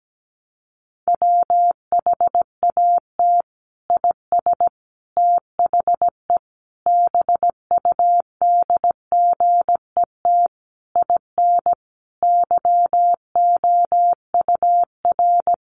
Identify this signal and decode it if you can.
Morse code, heard as a keyed tone, about 17 wpm, 700 Hz. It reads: WHAT IS THE BUDGET IN YOUR
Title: Sentences from Top 1200 Words 17wpm